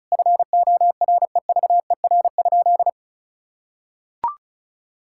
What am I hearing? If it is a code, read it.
Morse code, heard as a keyed tone, about 36 wpm, 700 Hz. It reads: FOREVER?
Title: Sentences from Top 1600 Words 35wpm